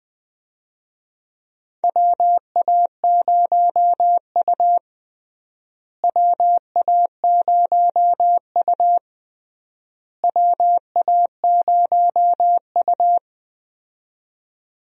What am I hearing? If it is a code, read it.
Morse code, heard as a keyed tone, about 20 wpm, 700 Hz. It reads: WA0U WA0U WA0U
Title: USA Call Signs - 4 Characters - Repeated 3x - 20wpm